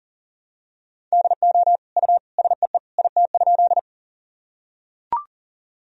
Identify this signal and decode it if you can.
Morse code, heard as a keyed tone, about 40 wpm, 700 Hz. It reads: DO U SEE IT?
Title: Sentences from Top 100 Words and Abbreviations 40wpm